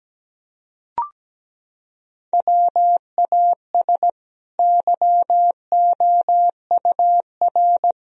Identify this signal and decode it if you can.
Morse code, heard as a keyed tone, about 17 wpm, 700 Hz. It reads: WAS YOUR
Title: Sentences from Top 1000 Words - Review Part 4 - 17wpm